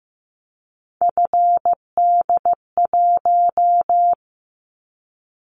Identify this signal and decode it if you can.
Morse code, heard as a keyed tone, about 15 wpm, 700 Hz. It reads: FD1
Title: Callsign Prefixes for Top 25 DXCC - 15wpm